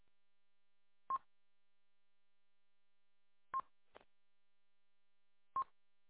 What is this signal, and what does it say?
There is no code.